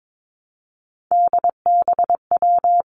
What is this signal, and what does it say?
Morse code, heard as a keyed tone, about 22 wpm, 700 Hz. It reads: DBW
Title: Three Challenging Letter-Number Combinations 22wpm